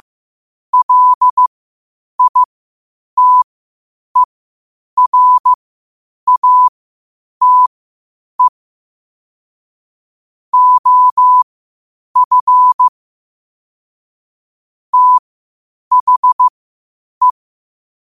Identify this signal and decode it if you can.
Morse code, heard as a keyed tone, about 15 wpm, spaced as for 8 wpm, 1000 Hz. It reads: LITERATE OF THE
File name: 8wpm